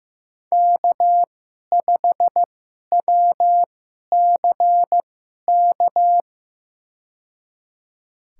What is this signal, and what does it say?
Morse code, heard as a keyed tone, about 15 wpm, spaced as for 11 wpm, 700 Hz. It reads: K5WCK
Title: USA Call Signs - 5 Characters - 2x Character Spacing - 15wpm